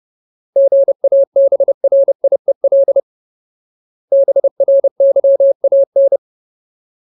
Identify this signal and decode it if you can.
Morse code, heard as a keyed tone, about 30 wpm, 550 Hz. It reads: GABRIEL BRYAN
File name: Ultimate Word List / 30WPM Ultimate